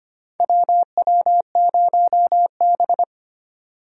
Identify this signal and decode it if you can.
Morse code, heard as a keyed tone, about 25 wpm, 700 Hz. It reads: WW0B
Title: USA Call Signs - 4 Characters - 25wpm